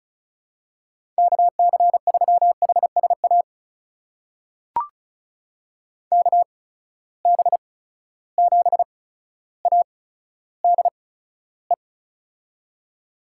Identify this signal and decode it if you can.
Morse code, heard as a keyed tone, about 36 wpm, spaced as for 15 wpm, 700 Hz. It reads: KC3HSA KB7ADE
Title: USA Call Signs - 6 Characters - 8x Character Spacing - 35wpm